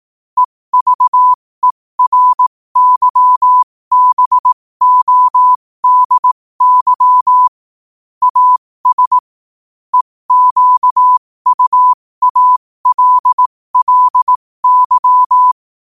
Morse code, heard as a keyed tone, about 18 wpm, spaced as for 15 wpm, 1000 Hz. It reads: EVERYBODY AS EQUALLY